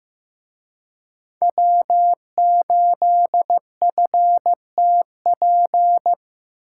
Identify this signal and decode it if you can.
Morse code, heard as a keyed tone, about 18 wpm, 700 Hz. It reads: W8FTP
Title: Speed Racing - USA 1x3 Call Signs 15wpm